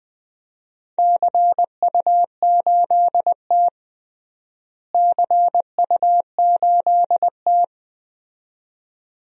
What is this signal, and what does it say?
Morse code, heard as a keyed tone, about 20 wpm, 700 Hz. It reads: CU8T CU8T
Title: DXCC-100 Call Signs - 4 Characters - Repeated 2x - 20wpm